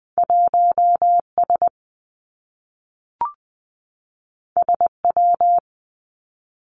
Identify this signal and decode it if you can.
Morse code, heard as a keyed tone, about 20 wpm, 700 Hz. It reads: 1 S S W
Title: Two Challenging Letter-Number Combinations 20wpm